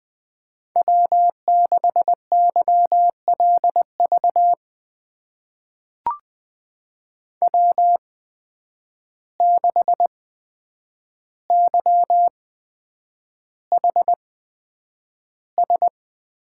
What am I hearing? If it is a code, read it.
Morse code, heard as a keyed tone, about 20 wpm, spaced as for 9 wpm, 700 Hz. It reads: W6YLV W6YHS
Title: USA Call Signs - 5 Characters - 8x Character Spacing - 20wpm